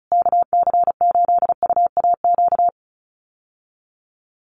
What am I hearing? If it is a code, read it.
Morse code, heard as a keyed tone, about 34 wpm, 700 Hz. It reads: KC8UAQ
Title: USA Call Signs Repeated 3x 35wpm